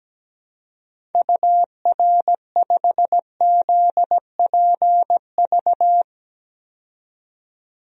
Morse code, heard as a keyed tone, about 17 wpm, 700 Hz. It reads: UR5ZPV
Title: DXCC-100 Call Signs - 6 Characters - 17wpm